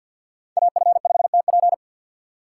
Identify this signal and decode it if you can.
Morse code, heard as a keyed tone, about 48 wpm, 700 Hz. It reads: AF5TP
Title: USA Call Signs - 5 Characters - 6x Character Spacing - 50wpm